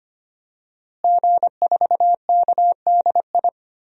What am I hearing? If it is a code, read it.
Morse code, heard as a keyed tone, about 25 wpm, 700 Hz. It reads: G4KDI
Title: DXCC-100 Call Signs - 5 Characters - Repeated 3x - 25wpm